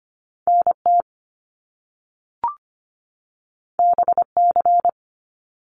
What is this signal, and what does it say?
Morse code, heard as a keyed tone, about 25 wpm, 700 Hz. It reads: NT BC
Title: Canadian Provinces and Territories 25wpm